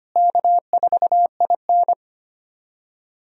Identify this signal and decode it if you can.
Morse code, heard as a keyed tone, about 25 wpm, 700 Hz. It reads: K4IN
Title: USA Call Signs - 4 Characters - Repeated 3x - 25wpm